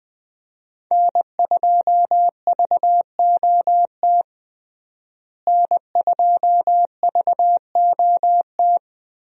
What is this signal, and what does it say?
Morse code, heard as a keyed tone, about 20 wpm, 700 Hz. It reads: N2VOT N2VOT
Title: USA Call Signs - 5 Characters - Repeated 3x - 20wpm